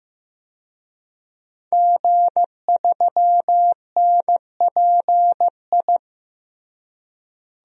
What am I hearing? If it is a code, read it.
Morse code, heard as a keyed tone, about 16 wpm, 700 Hz. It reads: G3NPI
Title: DXCC-100 Call Signs - 5 Characters - Repeated 2x - 15wpm